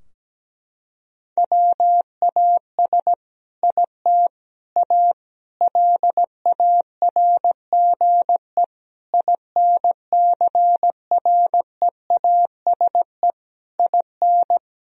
Morse code, heard as a keyed tone, about 17 wpm, 700 Hz. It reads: WAS IT A LARGE INCREASE IN
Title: Sentences from Top 1000 Words - Review Part 4 - 17wpm